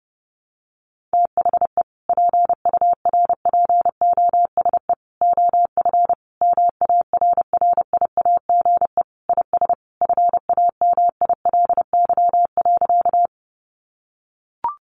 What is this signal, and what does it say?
Morse code, heard as a keyed tone, about 30 wpm, 700 Hz. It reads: THE PURPOSE OF MARRIAGE IS FAMILY.
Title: Sentences from Top 800 Words 30wpm